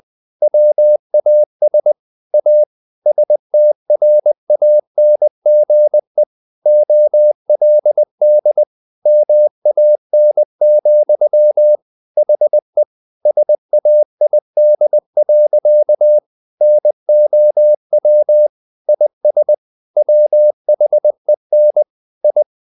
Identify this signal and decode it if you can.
Morse code, heard as a keyed tone, about 20 wpm, 600 Hz. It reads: WAS A STRANGE OLD MAN, HE SAID. NOW IS WHEN I
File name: the old man and the sea / the old man and the sea part 1 20wpm